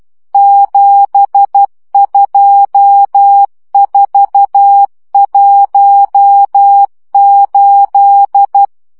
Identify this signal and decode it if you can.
Morse code, heard as a keyed tone, about 12 wpm, 800 Hz. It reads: 72418